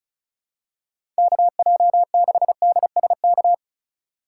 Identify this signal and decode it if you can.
Morse code, heard as a keyed tone, about 34 wpm, 700 Hz. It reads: KJ6DSK
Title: USA Call Signs - 3x Character Spacing - 35wpm